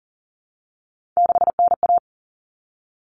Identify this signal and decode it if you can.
Morse code, heard as a keyed tone, about 40 wpm, 700 Hz. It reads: 6NA
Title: Common Cut Numbers - 3-Digit 40wpm